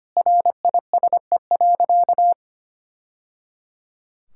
Morse code, heard as a keyed tone, about 25 wpm, 700 Hz. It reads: RISE.
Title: Sentences from Top 1000 Words - Review Part 6 - 25wpm